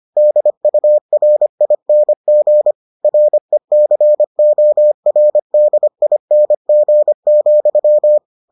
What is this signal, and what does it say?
Morse code, heard as a keyed tone, about 25 wpm, 600 Hz. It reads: DURING RECORDING,